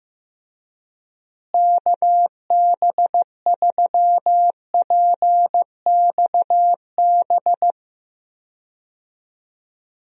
Morse code, heard as a keyed tone, about 15 wpm, 700 Hz. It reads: KB3PXB
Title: USA Call Signs - 6 Characters - Repeated 2x - 15wpm